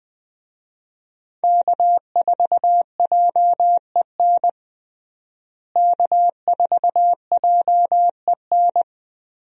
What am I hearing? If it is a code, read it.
Morse code, heard as a keyed tone, about 20 wpm, 700 Hz. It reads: K4JEN K4JEN
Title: USA Call Signs - 5 Characters - Repeated 3x - 20wpm